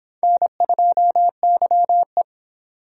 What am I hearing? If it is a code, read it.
Morse code, heard as a keyed tone, about 26 wpm, 700 Hz. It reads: N2YE